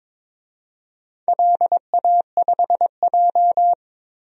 Morse code, heard as a keyed tone, about 23 wpm, 700 Hz. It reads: LA5J
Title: DXCC-100 Call Signs - 4 Characters - 22wpm